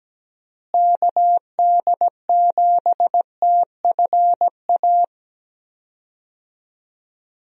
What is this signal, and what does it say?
Morse code, heard as a keyed tone, about 17 wpm, 700 Hz. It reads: KD7TFA
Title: USA Call Signs - 6 Characters - Repeated 3x - 17wpm